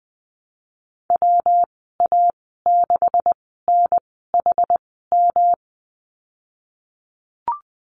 Morse code, heard as a keyed tone, about 19 wpm, spaced as for 14 wpm, 700 Hz. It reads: WA6NHM
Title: 154 Lesson 25g - Letter p - Call Signs 20wpm